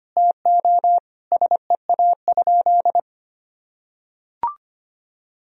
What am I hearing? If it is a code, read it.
Morse code, heard as a keyed tone, about 25 wpm, 700 Hz. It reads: TO SEA?
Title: Sentences from Top 1000 Words - Review Part 1 - 25wpm